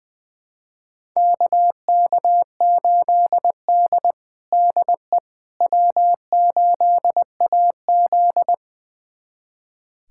Morse code, heard as a keyed tone, about 21 wpm, 700 Hz. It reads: KK8D DE W8AZ
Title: USA Callsign DE Callsign - 4 Characters - 20wpm